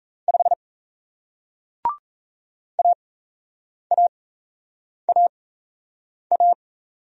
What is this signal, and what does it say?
Morse code, heard as a keyed tone, about 38 wpm, spaced as for 8 wpm, 700 Hz. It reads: 5 AAAA